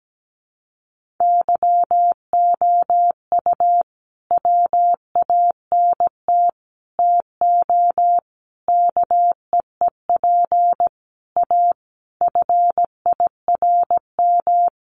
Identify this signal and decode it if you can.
Morse code, heard as a keyed tone, about 17 wpm, 700 Hz. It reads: YOU WANT TO KEEP A FIRM
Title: Sentences from Top 1400 Words 17wpm